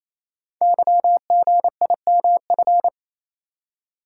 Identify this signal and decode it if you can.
Morse code, heard as a keyed tone, about 28 wpm, 700 Hz. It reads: YGIMF
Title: Code Groups - 5 Letter:Numbers - 28wpm